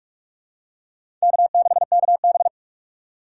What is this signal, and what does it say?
Morse code, heard as a keyed tone, about 44 wpm, 690 Hz. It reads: K6KB
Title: USA Call Signs - 4 Characters - 2x Character Spacing - 45wpm